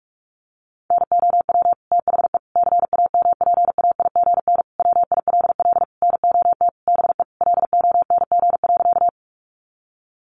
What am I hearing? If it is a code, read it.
Morse code, heard as a keyed tone, about 44 wpm, 700 Hz. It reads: NOW THE CAMPAIGN WILL NOT BE LONG.
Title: Sentences from Top 1200 Words 45wpm